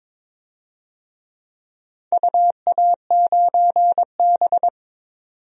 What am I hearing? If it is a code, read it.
Morse code, heard as a keyed tone, about 21 wpm, 690 Hz. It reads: UA9B
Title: DXCC-100 Call Signs - 4 Characters - 22wpm